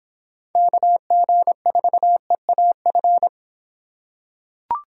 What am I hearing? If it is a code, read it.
Morse code, heard as a keyed tone, about 24 wpm, 700 Hz. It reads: KG4EAF